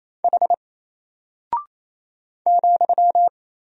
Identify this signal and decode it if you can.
Morse code, heard as a keyed tone, about 28 wpm, 700 Hz. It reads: H ,
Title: Single Character (Letter:Number:Punctuation:Prosign) Alt 28wpm